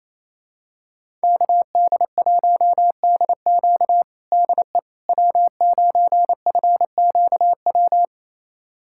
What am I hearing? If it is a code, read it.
Morse code, heard as a keyed tone, about 28 wpm, 700 Hz. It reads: KD1DQ DE W9FQW
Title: USA Callsign DE Callsign - 5 Characters - 28wpm